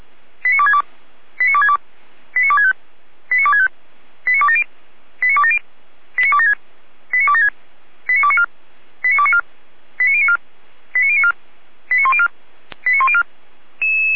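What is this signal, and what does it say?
Five-tone selective call (ZVEI), heard as quick runs of five tones, tones of about 70 ms. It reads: 89262 89262 89265 89265 89270 89270 89276 89276 89294 89294 89094 89094 89104 89104
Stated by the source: ZVEI five-tone call 89280, received on an SDR.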